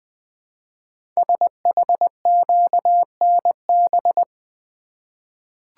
Morse code, heard as a keyed tone, about 20 wpm, 700 Hz. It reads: SHQNB
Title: Code Groups - 5 Letters - 20wpm